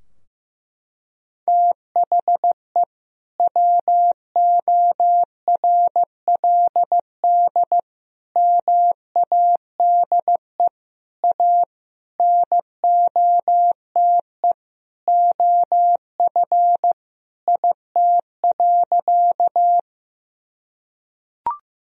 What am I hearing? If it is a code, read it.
Morse code, heard as a keyed tone, about 16 wpm, 700 Hz. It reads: THE WORLD MADE A NOTE OF IT.
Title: Sentences from Top 1000 Words - Review Part 2 - 15wpm